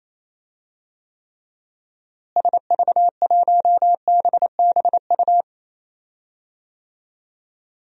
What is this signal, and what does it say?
Morse code, heard as a keyed tone, about 28 wpm, 700 Hz. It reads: SV1BBU
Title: DXCC-100 Call Signs - 6 Characters - Repeated 2x - 28wpm